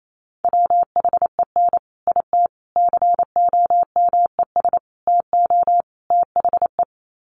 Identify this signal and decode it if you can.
Morse code, heard as a keyed tone, about 28 wpm, 700 Hz. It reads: WHEN IT COMES TO THE
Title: Sets of 5 Words 28wpm